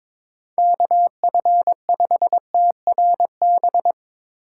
Morse code, heard as a keyed tone, about 22 wpm, 700 Hz. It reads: KF5TRB
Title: USA Call Signs - 6 Characters - 6x Character Spacing - 22wpm